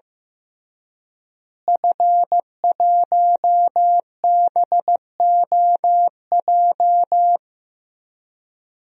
Morse code, heard as a keyed tone, about 15 wpm, 700 Hz. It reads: F1BOJ
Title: DXCC-100 Call Signs - 5 Characters - Repeated 3x - 15wpm